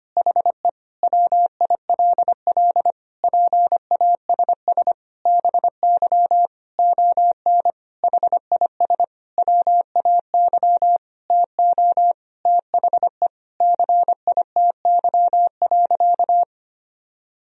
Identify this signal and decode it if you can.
Morse code, heard as a keyed tone, about 25 wpm, 700 Hz. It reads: HE WILL PASS BY ON HIS WAY TO THE CITY.
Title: Sentences from Top 1000 Words - Review Part 1 - 25wpm